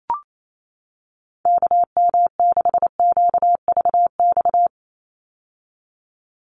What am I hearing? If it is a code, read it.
Morse code, heard as a keyed tone, about 28 wpm, 700 Hz. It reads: KM6QVX
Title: USA Call Signs 28wpm